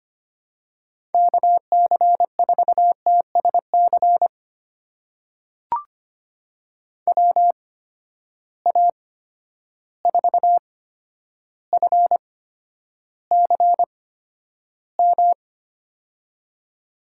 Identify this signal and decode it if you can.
Morse code, heard as a keyed tone, about 25 wpm, spaced as for 11 wpm, 700 Hz. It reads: KC4TSC WA4FCM
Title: USA Call Signs - 6 Characters - 8x Character Spacing - 25wpm